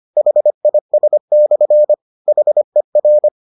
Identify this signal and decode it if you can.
Morse code, heard as a keyed tone, about 25 wpm, 600 Hz. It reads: HIS/HER